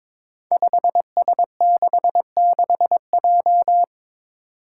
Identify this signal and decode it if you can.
Morse code, heard as a keyed tone, about 22 wpm, 700 Hz. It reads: 5S66J
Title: Five Challenging Letter-Number Combinations 22wpm